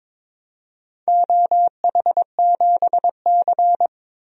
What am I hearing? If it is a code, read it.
Morse code, heard as a keyed tone, about 22 wpm, 700 Hz. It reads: OH7C
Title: DXCC-100 Call Signs - 4 Characters - 22wpm